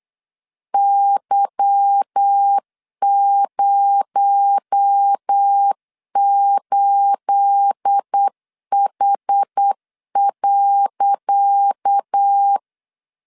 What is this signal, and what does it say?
Morse code, heard as a keyed tone, about 8 wpm, 800 Hz. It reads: Y08H.
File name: LEZIONI / 10 SIGNS / 10C DOT 8wpm 3649